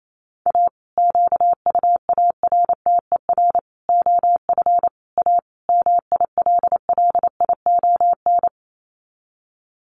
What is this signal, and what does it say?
Morse code, heard as a keyed tone, about 28 wpm, 700 Hz. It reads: A QUARTER OF A MILLION
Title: Sets of 5 Words Encore2 28wpm